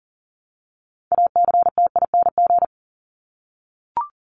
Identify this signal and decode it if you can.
Morse code, heard as a keyed tone, about 40 wpm, 700 Hz. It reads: ACTING
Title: ING Suffix - Top 100 Words with Base Words 40wpm